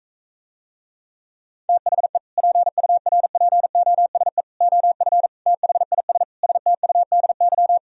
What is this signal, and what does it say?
Morse code, heard as a keyed tone, about 45 wpm, 700 Hz. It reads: THE PURPOSE OF THIS STUDY